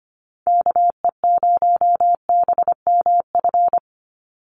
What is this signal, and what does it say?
Morse code, heard as a keyed tone, about 25 wpm, 690 Hz. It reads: KE0BMF
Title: USA Call Signs 25wpm